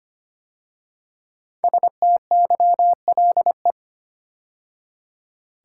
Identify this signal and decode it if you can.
Morse code, heard as a keyed tone, about 25 wpm, 700 Hz. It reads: STYLE
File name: 5 Letter Words - First 3 Letters Given - 25wpm